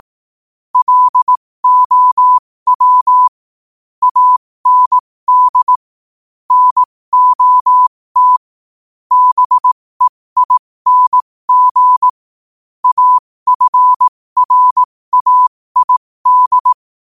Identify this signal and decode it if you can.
Morse code, heard as a keyed tone, about 18 wpm, spaced as for 15 wpm, 1000 Hz. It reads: LOW AND NOT BEING AFRAID
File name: qotdfiles / 15wpm